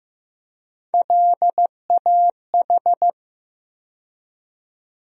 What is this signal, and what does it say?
Morse code, heard as a keyed tone, about 15 wpm, 700 Hz. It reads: LAH
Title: USA Call Sign Postfixes - 3 Letters 15wpm